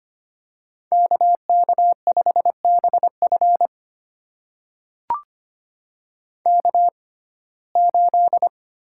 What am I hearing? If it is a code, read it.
Morse code, heard as a keyed tone, about 25 wpm, spaced as for 18 wpm, 700 Hz. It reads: KK5BF K8
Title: USA Call Signs - 6x Character Spacing - 25wpm